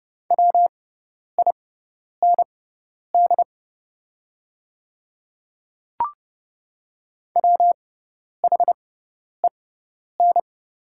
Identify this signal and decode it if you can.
Morse code, heard as a keyed tone, about 30 wpm, spaced as for 10 wpm, 690 Hz. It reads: WIND WHEN
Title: 6x Character Spacing - Top 500 Words - 30wpm